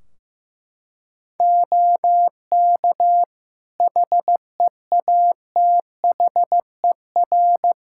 Morse code, heard as a keyed tone, about 15 wpm, 700 Hz. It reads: OK HEATHER
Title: K1USN SST QSO Practice 15wpm